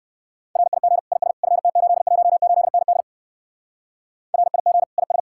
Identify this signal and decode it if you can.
Morse code, heard as a keyed tone, about 56 wpm, 700 Hz. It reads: RIG IS FT818ND RIG IS
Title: Speed-Racing Common QSO Phrases - 45wpm